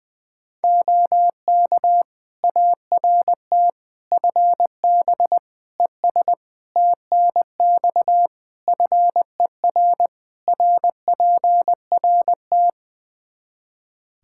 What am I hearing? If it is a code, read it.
Morse code, heard as a keyed tone, about 21 wpm, 700 Hz. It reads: OK ART FB ES TNX FER RPRT
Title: Practice QSO - Exchange 2 - Info - 20wpm